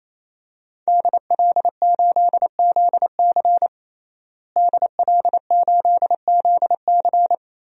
Morse code, heard as a keyed tone, about 28 wpm, 700 Hz. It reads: DL8ZC DL8ZC
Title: DXCC-100 Call Signs - 5 Characters - Repeated 3x - 28wpm